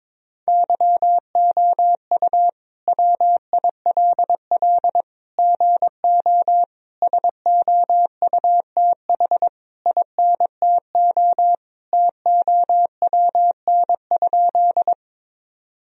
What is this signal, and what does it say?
Morse code, heard as a keyed tone, about 22 wpm, 700 Hz. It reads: YOU WILL GO SOUTH INTO TOWN?
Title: Sentences from Top 1000 Words - Review Part 2 - 22wpm